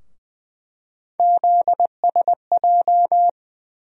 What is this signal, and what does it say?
Morse code, heard as a keyed tone, about 20 wpm, 700 Hz. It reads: ZSJ
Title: USA Call Sign Postfixes - 3 Letters 20wpm